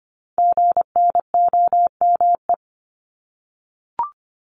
Morse code, heard as a keyed tone, about 25 wpm, 700 Hz. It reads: GNOME
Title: Multi-letter Phonemes 25wpm